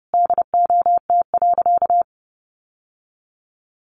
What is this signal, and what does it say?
Morse code, heard as a keyed tone, about 30 wpm, 700 Hz. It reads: DOT.
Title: Sentences from Top 1500 Words 30wpm